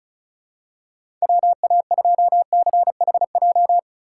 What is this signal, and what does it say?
Morse code, heard as a keyed tone, about 36 wpm, 700 Hz. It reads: WA2CHJ
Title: USA Call Signs - 4x Character Spacing - 35wpm